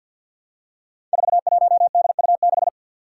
Morse code, heard as a keyed tone, about 51 wpm, 700 Hz. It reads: 41DU6
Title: Five Challenging Letter-Number Combinations 50wpm